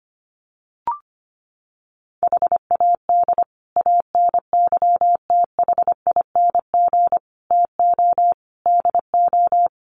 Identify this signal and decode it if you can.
Morse code, heard as a keyed tone, about 25 wpm, 700 Hz. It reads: HAD ANYTHING TO DO
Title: Sets of 5 Words Encore 25wpm